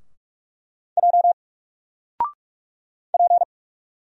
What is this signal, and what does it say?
Morse code, heard as a keyed tone, about 44 wpm, 700 Hz. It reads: J P
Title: Single Character (Letter:Number:Punctuation:Prosign) Alt 45wpm